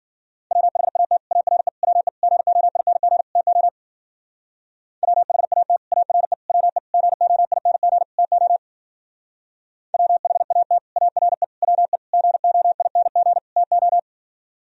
Morse code, heard as a keyed tone, about 49 wpm, 700 Hz. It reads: WHAT ARE WE GOING TO WHAT ARE WE GOING TO WHAT ARE WE GOING TO
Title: Speed-Racing - Sets of 5 Words 40wpm